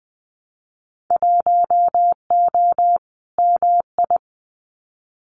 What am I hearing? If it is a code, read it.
Morse code, heard as a keyed tone, about 20 wpm, 700 Hz. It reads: 1O MI
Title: Winter Field Day Exchanges 20wpm